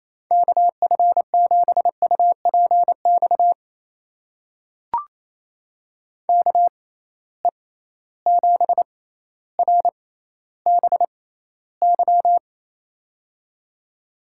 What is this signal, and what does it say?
Morse code, heard as a keyed tone, about 28 wpm, spaced as for 14 wpm, 700 Hz. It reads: KF7UPX KE7RBY
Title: USA Call Signs - 6 Characters - 6x Character Spacing - 28wpm